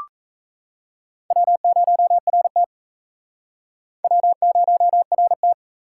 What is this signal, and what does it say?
Morse code, heard as a keyed tone, about 37 wpm, 700 Hz. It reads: W0RT W0RT